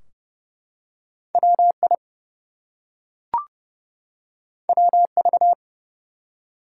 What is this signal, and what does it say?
Morse code, heard as a keyed tone, about 29 wpm, 700 Hz. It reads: WI WV
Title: US States and Canadian Provinces 30wpm